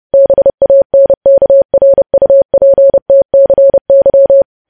Morse code, heard as a keyed tone, about 30 wpm, 550 Hz. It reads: BANKRUPTCY
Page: news headlines in morse code at 30 wpm